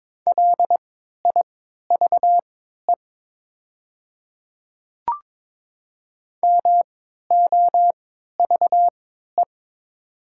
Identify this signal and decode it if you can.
Morse code, heard as a keyed tone, about 22 wpm, spaced as for 12 wpm, 700 Hz. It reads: LIVE MOVE
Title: 3x Character Spacing - Top 500 Words - 22wpm